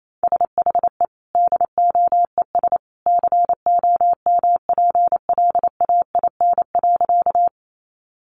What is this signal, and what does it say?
Morse code, heard as a keyed tone, about 28 wpm, 700 Hz. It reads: SHE DOES COMPLAIN.
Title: Sentences from Top 1300 Words 28wpm